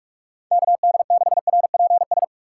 Morse code, heard as a keyed tone, about 46 wpm, 700 Hz. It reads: KD6RPS
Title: USA Call Signs - 6 Characters - 6x Character Spacing - 45wpm